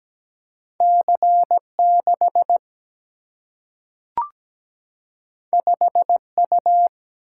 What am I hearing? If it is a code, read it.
Morse code, heard as a keyed tone, about 17 wpm, 700 Hz. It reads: C6 5U
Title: Single-Letter Single-Number 17wpm